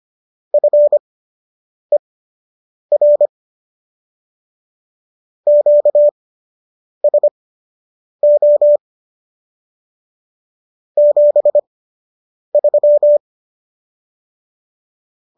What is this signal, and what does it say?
Morse code, heard as a keyed tone, about 25 wpm, spaced as for 8 wpm, 590 Hz. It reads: FER QSO 73